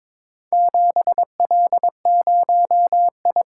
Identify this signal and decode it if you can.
Morse code, heard as a keyed tone, about 21 wpm, 700 Hz. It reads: 7L0I
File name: practice / 22wpm / Four Letter-Number Combinations - 22wpm